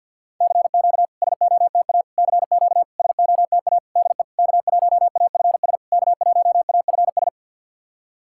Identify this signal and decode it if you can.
Morse code, heard as a keyed tone, about 53 wpm, 690 Hz. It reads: CQ SOTA CQ SOTA DE K1AFS K1AFS
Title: SOTA QSO Practice 50wpm